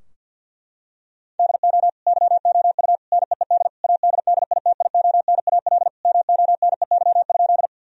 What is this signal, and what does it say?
Morse code, heard as a keyed tone, about 50 wpm, 700 Hz. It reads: DO YOU NEED ADDITIONAL MONEY?
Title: Sentences from Top 1000 Words - Review Part 4 - 50wpm